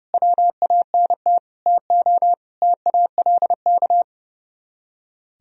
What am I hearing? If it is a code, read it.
Morse code, heard as a keyed tone, about 38 wpm, 700 Hz. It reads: WANT TO TALK